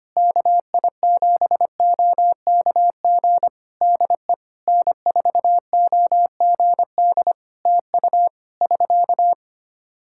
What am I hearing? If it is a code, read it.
Morse code, heard as a keyed tone, about 25 wpm, 700 Hz. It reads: KI7OKG DE N4OGD TU <SK>
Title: K1USN SST QSO Practice 25wpm